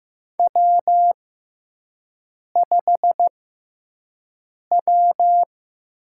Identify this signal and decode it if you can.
Morse code, heard as a keyed tone, about 15 wpm, spaced as for 5 wpm, 700 Hz. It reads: W5W
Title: USA Call Signs - 5 Characters - 6x Character Spacing - 15wpm